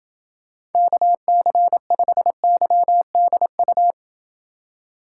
Morse code, heard as a keyed tone, about 27 wpm, 700 Hz. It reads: KC5YDU